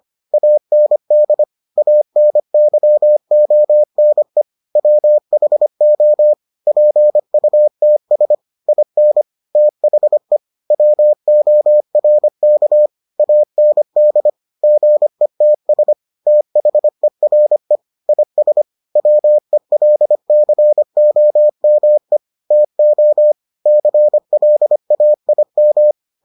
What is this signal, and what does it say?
Morse code, heard as a keyed tone, about 25 wpm, 600 Hz. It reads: AND ANYONE WHO PUTS IN THE WORK AND GETS THERE IS WELCOME TO CLAIM